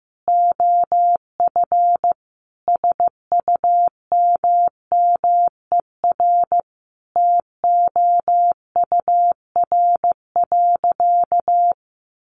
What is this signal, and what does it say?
Morse code, heard as a keyed tone, about 14 wpm, 700 Hz. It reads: OF SUMMER TOUR.
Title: Sentences from Top 1400 Words 15wpm